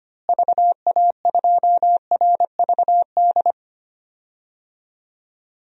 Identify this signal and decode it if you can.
Morse code, heard as a keyed tone, about 25 wpm, 700 Hz. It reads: VA2RVD
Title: DXCC-100 Call Signs - 6 Characters - Repeated 2x - 25wpm